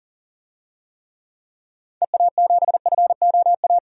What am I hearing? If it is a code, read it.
Morse code, heard as a keyed tone, about 40 wpm, 700 Hz. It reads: EA7FOA
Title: DXCC-100 Call Signs - 6 Characters - 40wpm